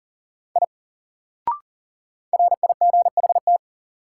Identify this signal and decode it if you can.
Morse code, heard as a keyed tone, about 40 wpm, 700 Hz. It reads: I RIGHT
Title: Sentences from Top 200 Words - New Word Drill - 40wpm